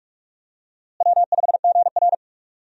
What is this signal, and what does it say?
Morse code, heard as a keyed tone, about 46 wpm, 700 Hz. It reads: W5GR
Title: USA Call Signs - 4 Characters - Repeated 3x - 45wpm